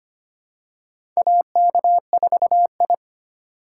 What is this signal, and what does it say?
Morse code, heard as a keyed tone, about 25 wpm, 700 Hz. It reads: AK4I
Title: USA Call Signs - 4 Characters - 4x Character Spacing - 25wpm